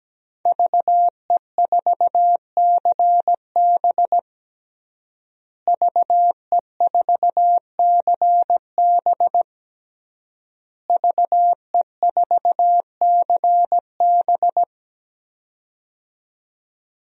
Morse code, heard as a keyed tone, about 17 wpm, 700 Hz. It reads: VE4CB VE4CB VE4CB
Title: DXCC-100 Call Signs - 5 Characters - Repeated 3x - 17wpm